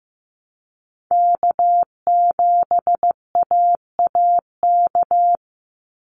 Morse code, heard as a keyed tone, about 15 wpm, 700 Hz. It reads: K7AAK
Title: USA Call Signs 15wpm